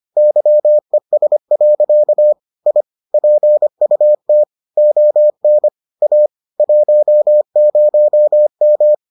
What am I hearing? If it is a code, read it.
Morse code, heard as a keyed tone, about 25 wpm, 600 Hz. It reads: YES. I PUT ON A 10M